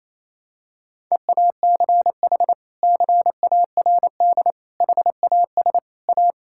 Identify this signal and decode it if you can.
Morse code, heard as a keyed tone, about 28 wpm, 700 Hz. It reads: EACH CARD HAS A
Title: Sentences from Top 1000 Words - Review Part 6 - 28wpm